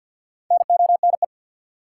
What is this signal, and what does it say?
Morse code, heard as a keyed tone, about 53 wpm, 700 Hz. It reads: NONE
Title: Four Letter Words - 50wpm